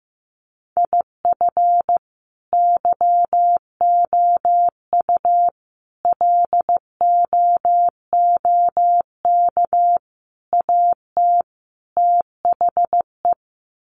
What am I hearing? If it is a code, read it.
Morse code, heard as a keyed tone, about 15 wpm, 700 Hz. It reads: IF YOU LOOK AT THE
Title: Sets of 5 Words 15wpm